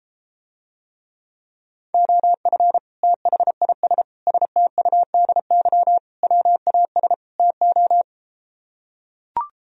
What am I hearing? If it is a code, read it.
Morse code, heard as a keyed tone, about 34 wpm, 700 Hz. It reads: OF THIS STUDY WAS TO